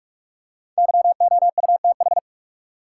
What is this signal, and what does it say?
Morse code, heard as a keyed tone, about 46 wpm, 700 Hz. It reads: YOUTH
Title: Five Letter Words - 45wpm